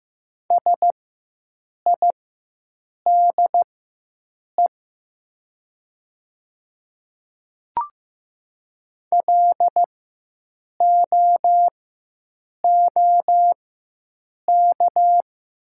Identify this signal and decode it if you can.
Morse code, heard as a keyed tone, about 15 wpm, spaced as for 7 wpm, 700 Hz. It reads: SIDE LOOK
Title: 4x Character Spacing - Top 100 Words - 15wpm